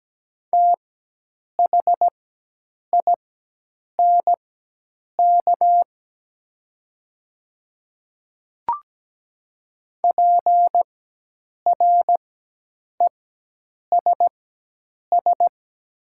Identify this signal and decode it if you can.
Morse code, heard as a keyed tone, about 17 wpm, spaced as for 8 wpm, 700 Hz. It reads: THINK PRESS
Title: 4x Character Spacing - Top 500 Words - 17wpm